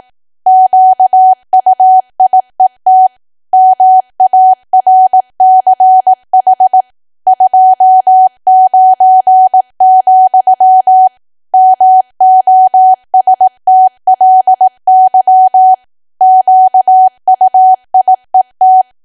Morse code, heard as a keyed tone, about 18 wpm, 750 Hz. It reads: QUIET MARCH 29, MOSTLY QUIET
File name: cwa / W1AW-Level2 / 18WPM